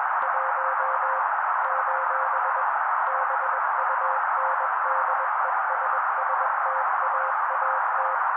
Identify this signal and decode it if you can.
Morse code, heard as a keyed tone, about 21 wpm, 530 Hz. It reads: 1 : BUNDESSTAAT
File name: daily / 20wpm noise